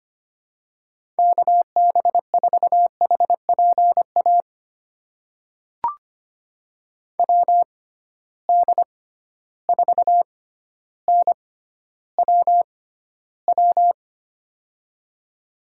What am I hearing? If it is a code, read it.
Morse code, heard as a keyed tone, about 25 wpm, spaced as for 13 wpm, 700 Hz. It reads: KB4HPA WD4NWW
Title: USA Call Signs - 6 Characters - 6x Character Spacing - 25wpm